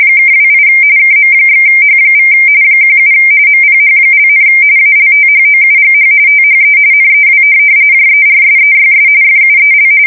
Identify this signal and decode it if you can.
RTTY, about 45.45 baud. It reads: (28 KM/H= 17 MPH), ALTHOUGH THE NEW JERSEYS FUEL
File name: fsk170 / USS-Missouri